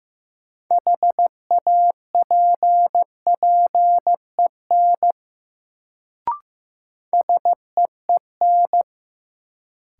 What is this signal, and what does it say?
Morse code, heard as a keyed tone, about 15 wpm, 700 Hz. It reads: HAPPEN SEEN
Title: Sets of 5 Words Encore - New Word Drill - 15wpm